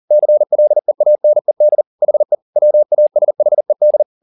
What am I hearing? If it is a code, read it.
Morse code, heard as a keyed tone, about 40 wpm, 600 Hz. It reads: CLEANED HE WASHED
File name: the old man and the sea 2 40wpm